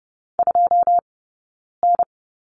Morse code, heard as a keyed tone, about 30 wpm, spaced as for 9 wpm, 690 Hz. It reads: 2N
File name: course / 30:10wpm / 086 Lesson 15d - Number 5 - Two Characters 30wpm